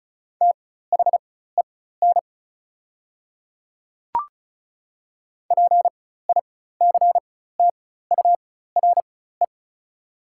Morse code, heard as a keyed tone, about 36 wpm, spaced as for 16 wpm, 690 Hz. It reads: THEN PICTURE
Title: 4x Character Spacing - Top 200 Words - 35wpm